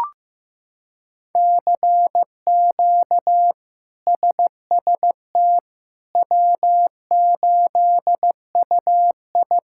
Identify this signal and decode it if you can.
Morse code, heard as a keyed tone, about 15 wpm, 700 Hz. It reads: CQ SST W8UI
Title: K1USN SST QSO Practice - 15wpm